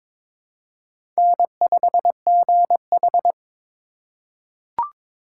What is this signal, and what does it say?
Morse code, heard as a keyed tone, about 22 wpm, 690 Hz. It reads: N5GH
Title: USA Call Signs - 4 Characters - 3x Character Spacing - 22wpm